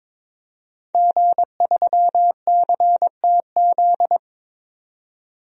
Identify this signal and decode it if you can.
Morse code, heard as a keyed tone, about 22 wpm, 700 Hz. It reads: G3CTZ
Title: DXCC-100 Call Signs - 5 Characters - 22wpm